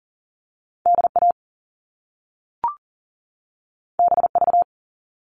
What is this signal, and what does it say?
Morse code, heard as a keyed tone, about 39 wpm, 700 Hz. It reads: DA BV
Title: All Cut Numbers - 2-Digit 40wpm